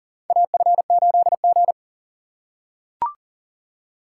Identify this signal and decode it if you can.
Morse code, heard as a keyed tone, about 40 wpm, 700 Hz. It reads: AF8G
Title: USA Call Signs - 4 Characters - 2x Character Spacing - 40wpm